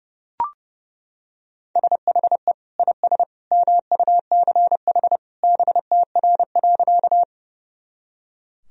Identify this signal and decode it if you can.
Morse code, heard as a keyed tone, about 30 wpm, 700 Hz. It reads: SHE IS MUCH BTR.
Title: Sentences from Top 400 Words and Abbreviations 30wpm